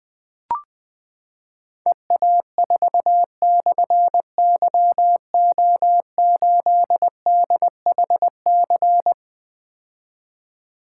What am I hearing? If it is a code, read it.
Morse code, heard as a keyed tone, about 20 wpm, 700 Hz. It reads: EA4/YO8DHC
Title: Difficult Call Signs 20wpm